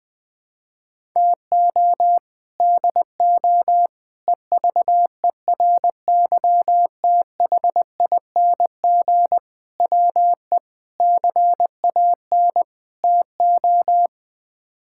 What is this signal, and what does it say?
Morse code, heard as a keyed tone, about 20 wpm, 700 Hz. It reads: TO DO EVERYTHING WE CAN TO
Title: Sets of 6 Words - 20wpm